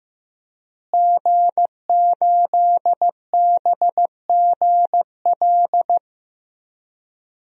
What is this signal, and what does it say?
Morse code, heard as a keyed tone, about 15 wpm, 700 Hz. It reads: G8BGL
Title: DXCC-100 Call Signs - 5 Characters - Repeated 3x - 15wpm